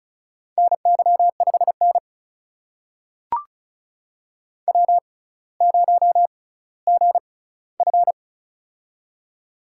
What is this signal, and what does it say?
Morse code, heard as a keyed tone, about 36 wpm, spaced as for 18 wpm, 700 Hz. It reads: NY5N W0GF
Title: USA Call Signs - 4 Characters - 6x Character Spacing - 35wpm